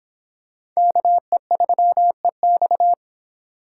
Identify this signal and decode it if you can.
Morse code, heard as a keyed tone, about 22 wpm, 700 Hz. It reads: KE3EX